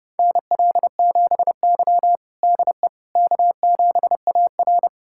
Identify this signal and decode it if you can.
Morse code, heard as a keyed tone, about 30 wpm, 700 Hz. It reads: NL7Y DE K7AR
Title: USA Callsign DE Callsign - 4 Characters - 30wpm